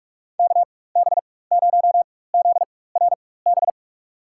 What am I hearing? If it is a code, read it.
Morse code, heard as a keyed tone, about 47 wpm, spaced as for 21 wpm, 700 Hz. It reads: KB0ZRB
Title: USA Call Signs - 6 Characters - 4x Character Spacing - 45wpm